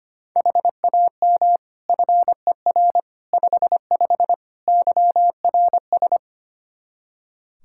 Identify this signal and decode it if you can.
Morse code, heard as a keyed tone, about 25 wpm, 700 Hz. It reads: HAM FER 55 YRS
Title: Practice QSO - Exchange 3 - More Info - 25wpm